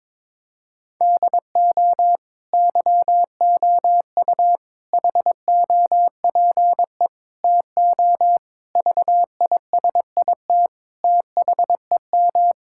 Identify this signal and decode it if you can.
Morse code, heard as a keyed tone, about 22 wpm, 700 Hz. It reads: DO YOU HOPE TO VISIT THEM
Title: Sentences from Top 1000 Words - Review Part 6 - 22wpm